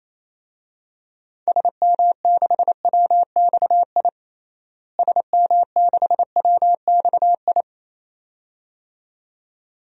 Morse code, heard as a keyed tone, about 28 wpm, 700 Hz. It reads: SM6WXI SM6WXI
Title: DXCC-100 Call Signs - 6 Characters - Repeated 2x - 28wpm